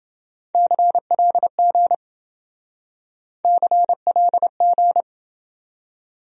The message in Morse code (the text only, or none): CLG CLG